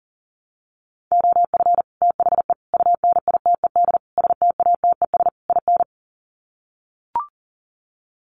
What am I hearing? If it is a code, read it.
Morse code, heard as a keyed tone, about 40 wpm, 700 Hz. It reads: OF THE UNITED STATES IN
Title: Sets of 5 Words Encore 40wpm